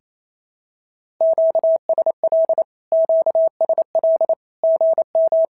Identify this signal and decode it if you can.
Morse code, heard as a keyed tone, about 28 wpm, 690 Hz. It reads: QSL QSL GM
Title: SOTA QSO Practice 28wpm